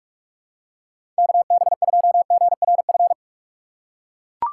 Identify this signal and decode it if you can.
Morse code, heard as a keyed tone, about 43 wpm, 700 Hz. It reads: KB2GRF
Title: USA Call Signs - 6 Characters - 8x Character Spacing - 45wpm